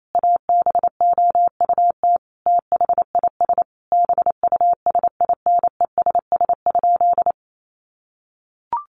Morse code, heard as a keyed tone, about 28 wpm, 700 Hz. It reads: ABOUT THIS BUSINESS?
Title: Sentences from Top 600 Words 28wpm